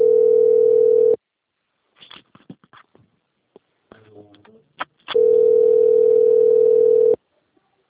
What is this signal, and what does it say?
Ringback tone (North America), 440 + 480 Hz.